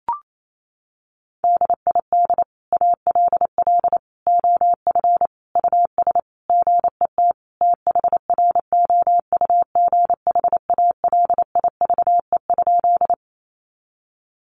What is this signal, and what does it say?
Morse code, heard as a keyed tone, about 28 wpm, 700 Hz. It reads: DID ALL OF US GET THROUGH ALIVE?
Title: A Sound Of Thunder 28wpm